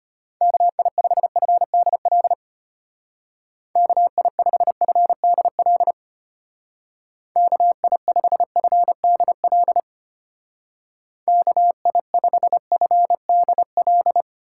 Morse code, heard as a keyed tone, about 32 wpm, 700 Hz. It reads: KI5FDL KI5FDL KI5FDL KI5FDL